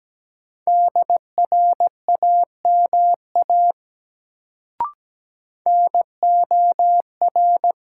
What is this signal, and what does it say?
Morse code, heard as a keyed tone, about 17 wpm, 700 Hz. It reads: DRAMA NOR
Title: Sentences from Top 1500 Words - New Word Drill Alt - 17wpm